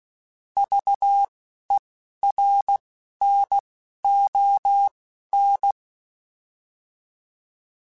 Morse code, heard as a keyed tone, about 16 wpm, spaced as for 11 wpm, 800 Hz. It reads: VERNON